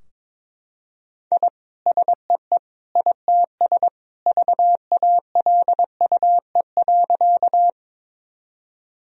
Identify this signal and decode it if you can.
Morse code, heard as a keyed tone, about 22 wpm, 700 Hz. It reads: I SEE ITS VALUE.
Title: Sentences from Top 1000 Words - Review Part 3 - 22wpm